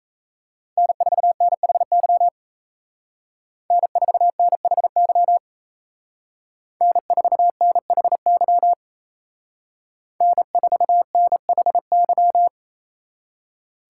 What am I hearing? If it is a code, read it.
Morse code, heard as a keyed tone, about 34 wpm, 700 Hz. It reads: N4NHY N4NHY N4NHY N4NHY
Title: Speed Racing - USA 1x3 Call Signs 28wpm